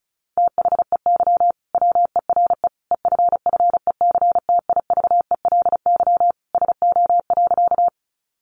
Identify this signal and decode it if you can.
Morse code, heard as a keyed tone, about 34 wpm, 700 Hz. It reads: THEY WERE EFFECTIVELY SO.
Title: Sentences from Top 1400 Words 35wpm